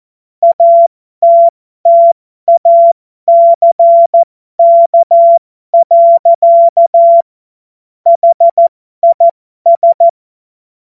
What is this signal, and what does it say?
Morse code, heard as a keyed tone, about 14 wpm, spaced as for 12 wpm, 680 Hz. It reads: ATTACK. HIS
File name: Morsekurs / Witze